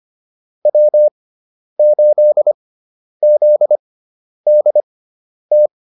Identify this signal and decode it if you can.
Morse code, heard as a keyed tone, about 25 wpm, spaced as for 10 wpm, 600 Hz. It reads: W8ZDT
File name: QSO108 10